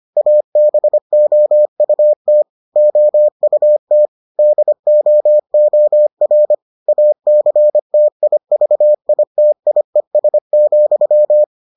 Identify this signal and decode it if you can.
Morse code, heard as a keyed tone, about 25 wpm, 600 Hz. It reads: ABOUT OUT DOOR ACTIVITIES,